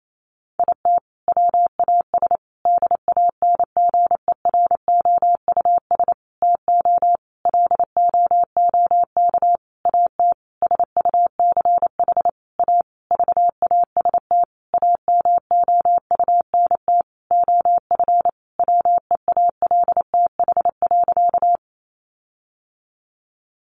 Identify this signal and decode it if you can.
Morse code, heard as a keyed tone, about 28 wpm, 700 Hz. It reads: IT WAS DANGEROUS TO LOOK AT SUCH A VAST AMOUNT OF WEALTH.
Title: Sentences from Top 1300 Words 28wpm